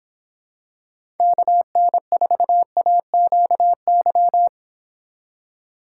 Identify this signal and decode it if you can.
Morse code, heard as a keyed tone, about 26 wpm, 700 Hz. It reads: KN4AQY